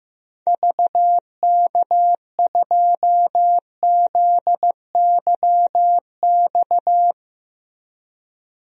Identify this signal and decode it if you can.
Morse code, heard as a keyed tone, about 16 wpm, 700 Hz. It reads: VK2ZYX
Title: DXCC-100 Call Signs - 6 Characters - Repeated 3x - 15wpm